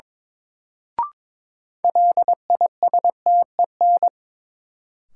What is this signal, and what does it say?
Morse code, heard as a keyed tone, about 22 wpm, 700 Hz. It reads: LISTEN
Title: Sentences from Top 400 Words - New Word Drill - 22wpm